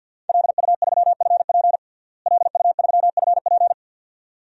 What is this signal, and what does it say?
Morse code, heard as a keyed tone, about 53 wpm, 700 Hz. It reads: LU3FP LU3FP
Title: DXCC-100 Call Signs - 5 Characters - Repeated 2x - 50wpm